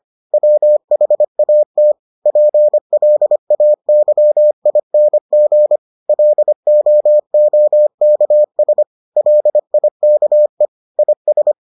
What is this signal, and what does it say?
Morse code, heard as a keyed tone, about 25 wpm, 600 Hz. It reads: WHAT PLAYING LOOKS LIKE IS